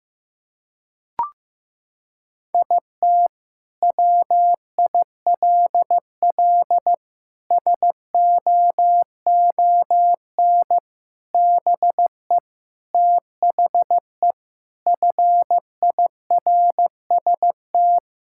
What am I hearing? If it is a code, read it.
Morse code, heard as a keyed tone, about 15 wpm, 700 Hz. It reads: IT WILL SOON BE THE FIRST
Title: Sentences from Top 1000 Words - Review Part 5 - 15wpm